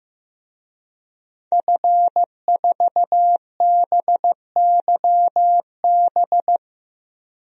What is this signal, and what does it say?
Morse code, heard as a keyed tone, about 15 wpm, 700 Hz. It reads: F4BYB
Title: DXCC-100 Call Signs - 5 Characters - 15wpm